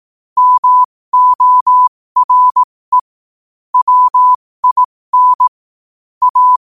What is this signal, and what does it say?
Morse code, heard as a keyed tone, about 18 wpm, spaced as for 15 wpm, 1000 Hz. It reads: MORE WIN A
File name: qotdfiles / 15wpm